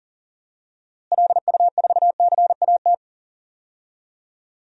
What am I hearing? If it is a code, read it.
Morse code, heard as a keyed tone, about 37 wpm, 700 Hz. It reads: LU4CAT
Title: DXCC-100 Call Signs - 6 Characters - Repeated 2x - 40wpm